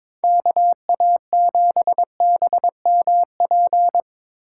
Morse code, heard as a keyed tone, about 22 wpm, 700 Hz. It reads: KA7BMP
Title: USA Call Signs - 6 Characters - 4x Character Spacing - 22wpm